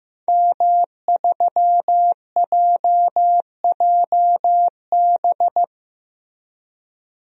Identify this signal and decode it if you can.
Morse code, heard as a keyed tone, about 15 wpm, 700 Hz. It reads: M3JJB
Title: DXCC-100 Call Signs - 5 Characters - 15wpm